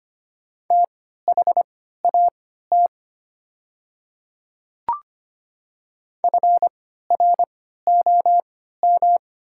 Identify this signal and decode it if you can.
Morse code, heard as a keyed tone, about 25 wpm, spaced as for 14 wpm, 700 Hz. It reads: THAT FROM
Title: 3x Character Spacing - Top 200 Words - 25wpm